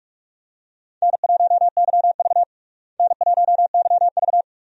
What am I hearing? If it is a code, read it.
Morse code, heard as a keyed tone, about 44 wpm, 700 Hz. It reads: N1YV N1YV
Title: USA Call Signs - 4 Characters - Repeated 2x - 45wpm